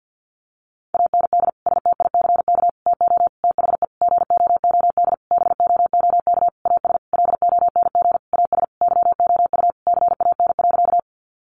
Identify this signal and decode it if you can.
Morse code, heard as a keyed tone, about 50 wpm, 700 Hz. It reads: AND STICK TO THE GOOD BOOK AS LONG AS YOU CAN.
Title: Sentences from Top 1200 Words 50wpm